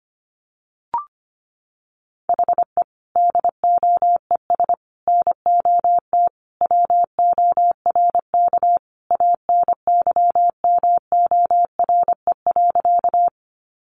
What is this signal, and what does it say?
Morse code, heard as a keyed tone, about 25 wpm, 700 Hz. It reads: HE DOES NOT WORK ANYMORE.
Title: Additional Easy Phrases 25wpm